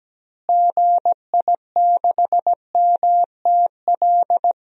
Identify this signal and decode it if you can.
Morse code, heard as a keyed tone, about 17 wpm, 700 Hz. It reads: GI6MTL
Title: DXCC-100 Call Signs - 6 Characters - Repeated 3x - 17wpm